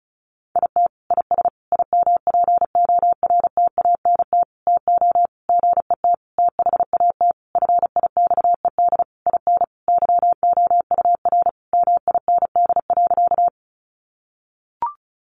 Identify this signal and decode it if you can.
Morse code, heard as a keyed tone, about 34 wpm, 700 Hz. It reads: IT IS IMPORTANT TO GET THAT FIXED IN YOUR MIND.
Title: Sentences from Top 1100 Words 35wpm